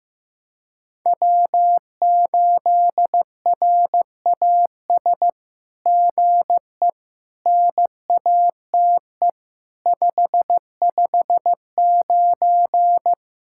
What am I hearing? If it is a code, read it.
Morse code, heard as a keyed tone, about 16 wpm, 700 Hz. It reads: W8RAS GE NATE 559
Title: SOTA QSO Practice 15wpm